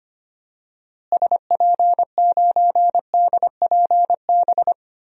Morse code, heard as a keyed tone, about 25 wpm, 700 Hz. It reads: SP9DPB
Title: DXCC-100 Call Signs - 6 Characters - Repeated 3x - 25wpm